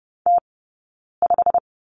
Morse code, heard as a keyed tone, about 30 wpm, 700 Hz. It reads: T 5
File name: Lesson 17d - Letter u - Two Characters 30wpm